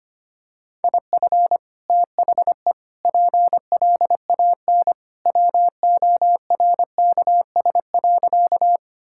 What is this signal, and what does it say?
Morse code, heard as a keyed tone, about 25 wpm, 700 Hz. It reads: IF THE PLAN WORKS.
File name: practice / 25wpm / Sentences with 2-Word Phrases - 25wpm